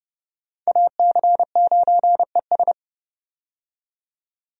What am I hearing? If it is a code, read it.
Morse code, heard as a keyed tone, about 30 wpm, 700 Hz. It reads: AC9ES
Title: USA Call Signs - 5 Characters - 30wpm